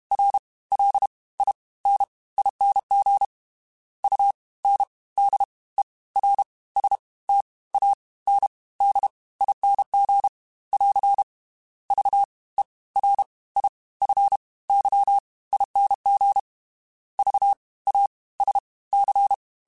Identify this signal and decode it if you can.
Morse code, heard as a keyed tone, about 32 wpm, spaced as for 20 wpm, 800 Hz. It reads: RLINING UNDERSTANDING <AR> VERIFYING VASC
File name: Special INGs / Special INGs-mp3s / Fast ing @30